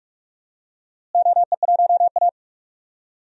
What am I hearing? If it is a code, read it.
Morse code, heard as a keyed tone, about 46 wpm, 700 Hz. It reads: OE1A
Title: DXCC-100 Call Signs - 4 Characters - Repeated 2x - 45wpm